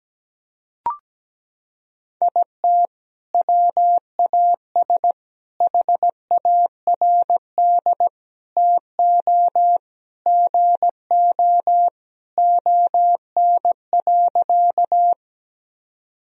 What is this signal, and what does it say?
Morse code, heard as a keyed tone, about 17 wpm, 700 Hz. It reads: IT WAS HARD TO GO ON.
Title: Sentences from Top 1000 Words - Review Part 1 - 17wpm